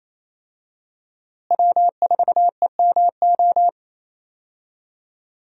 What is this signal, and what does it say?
Morse code, heard as a keyed tone, about 28 wpm, 700 Hz. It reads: W4EMO
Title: USA Call Signs - 5 Characters - Repeated 2x - 28wpm